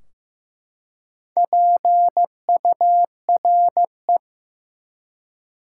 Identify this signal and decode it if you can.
Morse code, heard as a keyed tone, about 16 wpm, 700 Hz. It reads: PURE
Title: Four Letter Words - 15wpm